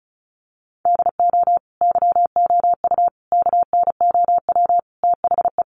Morse code, heard as a keyed tone, about 34 wpm, 700 Hz. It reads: DO YOU KNOW THE
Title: Sentences from Top 1000 Words 35wpm